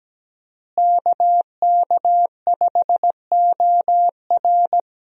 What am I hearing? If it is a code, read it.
Morse code, heard as a keyed tone, about 17 wpm, 700 Hz. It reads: KK5OR
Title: USA Call Signs - 5 Characters - Repeated 3x - 17wpm